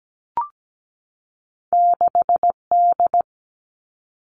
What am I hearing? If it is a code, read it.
Morse code, heard as a keyed tone, about 17 wpm, 700 Hz. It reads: 6 D
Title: Two Challenging Letter-Number Combinations 17wpm